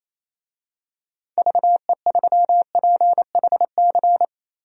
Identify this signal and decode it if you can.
Morse code, heard as a keyed tone, about 28 wpm, 700 Hz. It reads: VE3PHC
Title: DXCC-100 Call Signs - 6 Characters - 28wpm